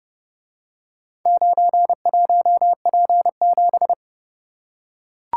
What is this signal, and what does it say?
Morse code, heard as a keyed tone, about 28 wpm, 700 Hz. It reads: 91P7